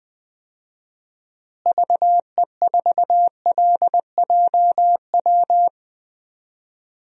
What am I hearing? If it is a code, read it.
Morse code, heard as a keyed tone, about 20 wpm, 700 Hz. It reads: VE4LJW
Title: DXCC-100 Call Signs - 6 Characters - Repeated 2x - 20wpm